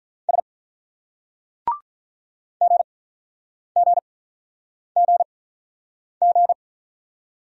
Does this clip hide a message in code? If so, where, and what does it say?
Yes: Morse code from 0.3 to 6.5 s, heard as a keyed tone, about 48 wpm, spaced as for 9 wpm, 700 Hz. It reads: S GGGG